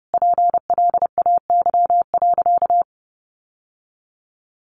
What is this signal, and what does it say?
Morse code, heard as a keyed tone, about 30 wpm, 700 Hz. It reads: PLAY.
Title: Sentences from Top 900 Words 30wpm